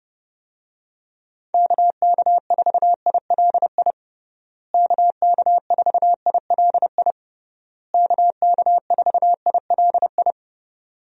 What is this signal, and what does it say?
Morse code, heard as a keyed tone, about 32 wpm, 700 Hz. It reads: KK4ILI KK4ILI KK4ILI
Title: USA Call Signs - 6 Characters - Repeated 3x - 30wpm